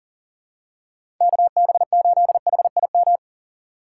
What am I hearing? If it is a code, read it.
Morse code, heard as a keyed tone, about 40 wpm, 700 Hz. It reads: KB8HIM
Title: USA Call Signs - 6 Characters - 2x Character Spacing - 40wpm